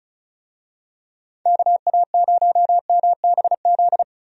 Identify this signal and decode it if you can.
Morse code, heard as a keyed tone, about 34 wpm, 700 Hz. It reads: KA0MBZ
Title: USA Call Signs - 6 Characters - Repeated 2x - 35wpm